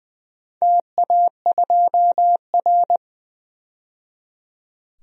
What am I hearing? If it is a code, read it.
Morse code, heard as a keyed tone, about 20 wpm, 700 Hz. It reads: TA2R
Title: DXCC-100 Call Signs - 4 Characters - 20wpm